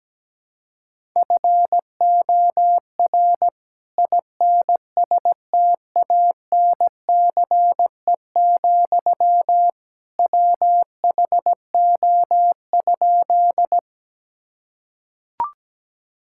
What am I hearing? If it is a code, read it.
Morse code, heard as a keyed tone, about 17 wpm, 700 Hz. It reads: FOR INSTANCE, WHO?
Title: Sentences from Top 1000 Words - Review Part 5 - 17wpm